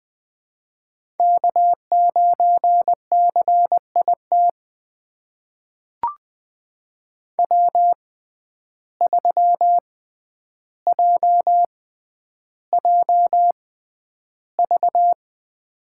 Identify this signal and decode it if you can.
Morse code, heard as a keyed tone, about 20 wpm, spaced as for 10 wpm, 700 Hz. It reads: K9CIT W3JJV
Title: USA Call Signs - 5 Characters - 6x Character Spacing - 20wpm